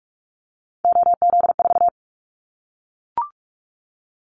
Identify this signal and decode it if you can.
Morse code, heard as a keyed tone, about 44 wpm, 700 Hz. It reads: OZ4
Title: Callsign Prefixes for Top 50 DXCC - 45wpm